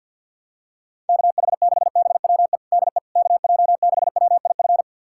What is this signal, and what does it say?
Morse code, heard as a keyed tone, about 45 wpm, 700 Hz. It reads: KH6BWE DE KJ6WIF
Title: USA Callsign DE Callsign 50wpm